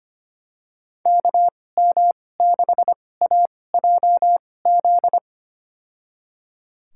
Morse code, heard as a keyed tone, about 25 wpm, spaced as for 18 wpm, 700 Hz. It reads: KM6AJZ
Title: USA Call Signs - 6 Characters - 2x Character Spacing - 25wpm